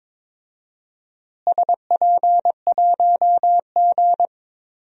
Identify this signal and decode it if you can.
Morse code, heard as a keyed tone, about 21 wpm, 700 Hz. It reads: SP1G
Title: DXCC-100 Call Signs - 4 Characters - Repeated 2x - 22wpm